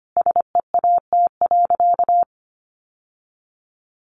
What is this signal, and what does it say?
Morse code, heard as a keyed tone, about 25 wpm, 700 Hz. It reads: SEAT.
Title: Sentences from Top 1500 Words 25wpm